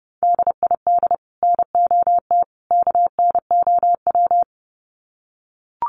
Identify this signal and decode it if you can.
Morse code, heard as a keyed tone, about 30 wpm, 700 Hz. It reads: DID NOT KNOW
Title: Sets of 3 Words 30wpm